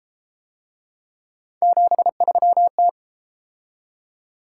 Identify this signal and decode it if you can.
Morse code, heard as a keyed tone, about 30 wpm, 700 Hz. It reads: 73T